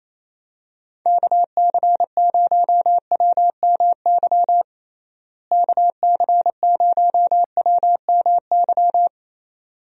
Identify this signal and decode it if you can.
Morse code, heard as a keyed tone, about 28 wpm, 700 Hz. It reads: KC0WMY KC0WMY
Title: USA Call Signs - 6 Characters - Repeated 2x - 28wpm